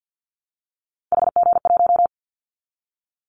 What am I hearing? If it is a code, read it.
Morse code, heard as a keyed tone, about 49 wpm, 700 Hz. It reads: HG1
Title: Callsign Prefixes for Top 25 DXCC - 50wpm